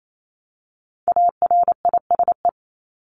Morse code, heard as a keyed tone, about 28 wpm, 700 Hz. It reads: ARISE
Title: Top 1501-1750 Words 28wpm